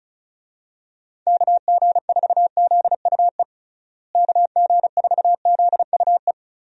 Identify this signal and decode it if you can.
Morse code, heard as a keyed tone, about 36 wpm, 700 Hz. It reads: KG4ZUE KG4ZUE
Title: USA Call Signs - 6 Characters - Repeated 2x - 35wpm